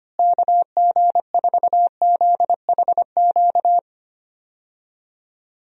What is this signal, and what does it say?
Morse code, heard as a keyed tone, about 25 wpm, 700 Hz. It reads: KG4ZHQ
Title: USA Call Signs - 6 Characters - Repeated 3x - 25wpm